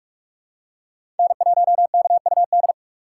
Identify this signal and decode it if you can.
Morse code, heard as a keyed tone, about 41 wpm, 700 Hz. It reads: N1KUD
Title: USA Call Signs - 5 Characters - 4x Character Spacing - 45wpm